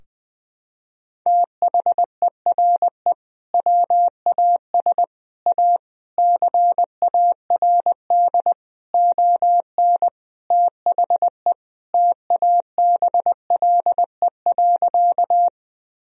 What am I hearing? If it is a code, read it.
Morse code, heard as a keyed tone, about 21 wpm, 700 Hz. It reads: THERE WAS A CARD ON THE TABLE.
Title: Sentences from Top 1000 Words - Review Part 4 - 20wpm